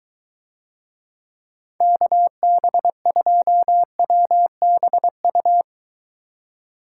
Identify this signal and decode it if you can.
Morse code, heard as a keyed tone, about 23 wpm, 700 Hz. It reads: KB2WBU